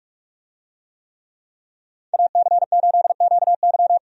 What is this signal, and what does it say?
Morse code, heard as a keyed tone, about 46 wpm, 700 Hz. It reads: AC8QY
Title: USA Call Signs - 5 Characters - 45wpm